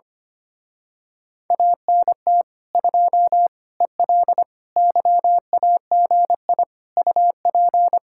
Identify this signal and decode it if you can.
Morse code, heard as a keyed tone, about 24 wpm, 700 Hz. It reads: ANT 2 EL YAGI UP
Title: Speed-Racing Common QSO Phrases Encore - 25wpm